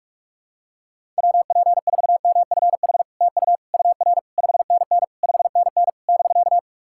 Morse code, heard as a keyed tone, about 48 wpm, 700 Hz. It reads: WP4MFH TU UR 5NN 5NN <BK>
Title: SOTA QSO Practice 45wpm